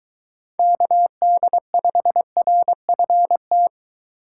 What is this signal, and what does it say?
Morse code, heard as a keyed tone, about 22 wpm, 700 Hz. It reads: KD5RFT